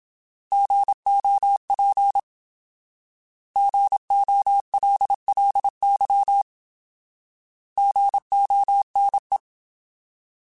Morse code, heard as a keyed tone, about 27 wpm, 800 Hz. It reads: GOP GOLLY GONE
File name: Alphabetical / Alphabetical mp3s / Gs @25